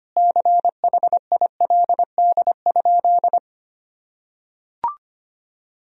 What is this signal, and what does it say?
Morse code, heard as a keyed tone, about 25 wpm, 700 Hz. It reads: CHILD?
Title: Sentences from Top 1000 Words - Review Part 2 - 25wpm